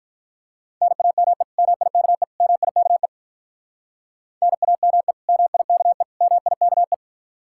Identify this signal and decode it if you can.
Morse code, heard as a keyed tone, about 45 wpm, 700 Hz. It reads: NAME MIKE MIKE NAME MIKE MIKE